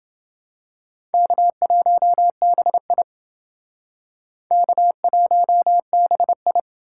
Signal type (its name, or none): Morse code